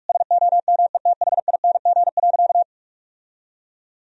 Morse code, heard as a keyed tone, about 41 wpm, 700 Hz. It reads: SOMETHING.
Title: Sentences from Top 1000 Words - Review Part 3 - 45wpm